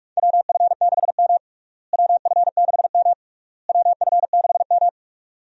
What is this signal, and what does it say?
Morse code, heard as a keyed tone, about 46 wpm, 700 Hz. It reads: WF6M WF6M WF6M
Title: USA Call Signs - 4 Characters - Repeated 3x - 45wpm